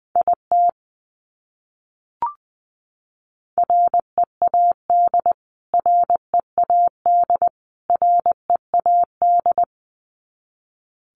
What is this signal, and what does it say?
Morse code, heard as a keyed tone, about 20 wpm, 700 Hz. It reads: IT READ READ READ
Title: Top 200 Words Repeated 20wpm